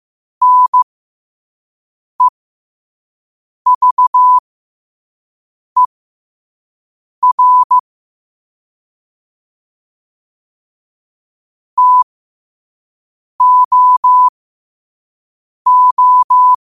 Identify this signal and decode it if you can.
Morse code, heard as a keyed tone, about 15 wpm, spaced as for 5 wpm, 1000 Hz. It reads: NEVER TOO
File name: qotdfiles / 5wpm